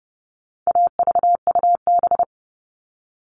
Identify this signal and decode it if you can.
Morse code, heard as a keyed tone, about 30 wpm, 700 Hz. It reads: AVUB
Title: All Cut Numbers - 4-Digit 30wpm